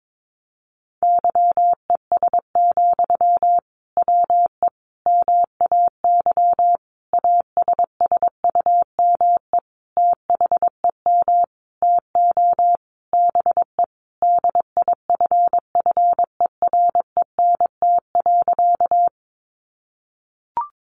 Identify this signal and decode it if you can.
Morse code, heard as a keyed tone, about 22 wpm, 700 Hz. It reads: YES, WE MAY ASSUME THEM TO BE DIFFERENT.
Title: Sentences from Top 1500 Words 22wpm